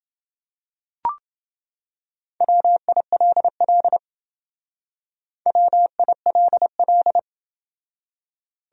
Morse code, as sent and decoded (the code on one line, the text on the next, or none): .-- .. .-.. .-.. / .-- .. .-.. .-..
WILL WILL